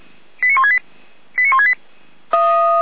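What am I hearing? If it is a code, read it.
Five-tone selective call (ZVEI), heard as quick runs of five tones, tones of about 70 ms. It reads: 89168 89168
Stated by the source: ZVEI five-tone selective call received with an SDR.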